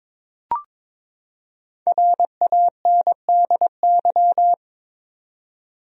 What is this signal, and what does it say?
Morse code, heard as a keyed tone, about 22 wpm, 700 Hz. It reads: RANDY
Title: USA Common Names - 22wpm